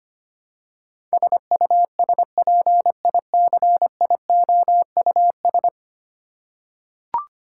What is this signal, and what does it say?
Morse code, heard as a keyed tone, about 25 wpm, 700 Hz. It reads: SUSPICIOUS
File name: 10 Letter Words - First 3 Letters Given - 25wpm